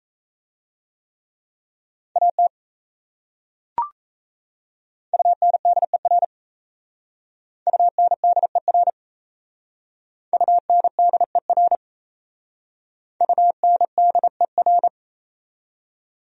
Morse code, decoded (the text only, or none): AT UNDER UNDER UNDER UNDER